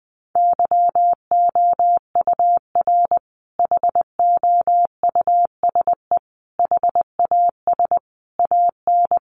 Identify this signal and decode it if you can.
Morse code, heard as a keyed tone, about 20 wpm, 700 Hz. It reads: YOUR HOUSE HAS AN
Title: Sentences from Top 1300 Words 20wpm